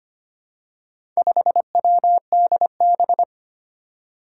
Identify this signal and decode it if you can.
Morse code, heard as a keyed tone, about 25 wpm, 700 Hz. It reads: 5WDB
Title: Four Challenging Letter-Number Combinations 25wpm